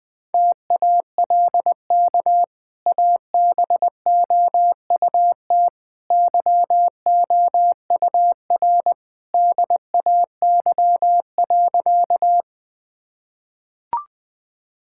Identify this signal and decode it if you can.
Morse code, heard as a keyed tone, about 20 wpm, 700 Hz. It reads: TALK ABOUT YOUR DAY.
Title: Easy Sentences with Hints - 20wpm